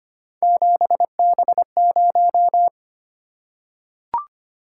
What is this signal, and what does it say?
Morse code, heard as a keyed tone, about 25 wpm, 700 Hz. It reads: 7B0
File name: Three Letter-Number Combinations - 25wpm